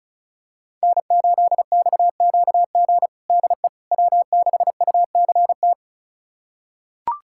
Morse code, heard as a keyed tone, about 34 wpm, 700 Hz. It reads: N8XQG DE W6UCT
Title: USA Callsign DE Callsign - 5 Characters - 35wpm